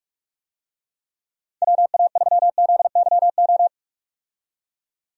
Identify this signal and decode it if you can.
Morse code, heard as a keyed tone, about 46 wpm, 700 Hz. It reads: WA3ZYO
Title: USA Call Signs - 6 Characters - 45wpm